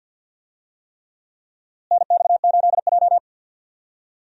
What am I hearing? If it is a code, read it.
Morse code, heard as a keyed tone, about 49 wpm, 700 Hz. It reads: NX8J
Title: USA Call Signs - 4 Characters - Repeated 3x - 50wpm